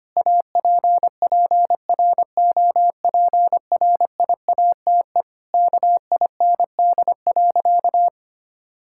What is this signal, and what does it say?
Morse code, heard as a keyed tone, about 25 wpm, 700 Hz. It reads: APPROPRIATE KIND.
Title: Sentences from Top 1000 Words - Review Part 5 - 25wpm